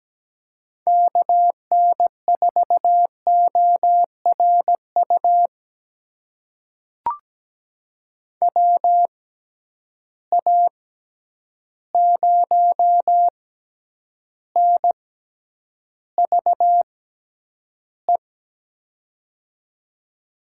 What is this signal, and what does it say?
Morse code, heard as a keyed tone, about 17 wpm, spaced as for 9 wpm, 700 Hz. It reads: KN4ORU WA0NVE
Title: USA Call Signs - 6 Characters - 6x Character Spacing - 17wpm